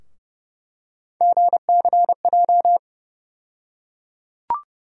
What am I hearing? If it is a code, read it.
Morse code, heard as a keyed tone, about 30 wpm, 700 Hz. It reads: GCJ
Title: USA Call Sign Postfixes - 3 Letters 30wpm